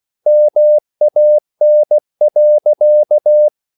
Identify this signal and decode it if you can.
Morse code, heard as a keyed tone, about 16 wpm, 600 Hz. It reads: MAN.